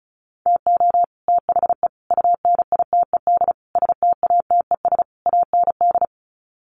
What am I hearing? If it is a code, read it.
Morse code, heard as a keyed tone, about 34 wpm, 700 Hz. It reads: TO THE UNITED STATES AND
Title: Sets of 5 Words Encore 35wpm